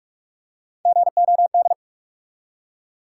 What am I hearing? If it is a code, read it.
Morse code, heard as a keyed tone, about 44 wpm, 700 Hz. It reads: GOD
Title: Three Letter Words - 45wpm